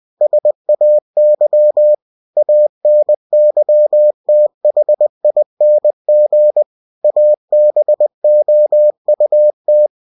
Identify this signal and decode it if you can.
Morse code, heard as a keyed tone, about 20 wpm, 600 Hz. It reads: SAY ANYTHING ABOUT
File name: cw-books / winnie-the-pooh / winnie-20wpm-page-0079